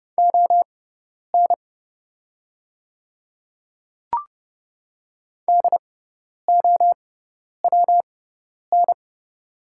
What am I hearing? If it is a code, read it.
Morse code, heard as a keyed tone, about 30 wpm, spaced as for 10 wpm, 700 Hz. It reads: ON DOWN
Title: 6x Character Spacing - Top 200 Words - 30wpm